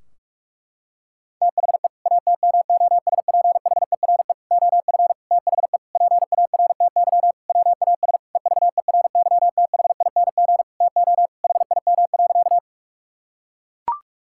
Morse code, heard as a keyed tone, about 41 wpm, 700 Hz. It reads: THE ATMOSPHERE OF THE PARTY WAS EVERYTHING TO HIM.
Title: Sentences from Top 1600 Words 45wpm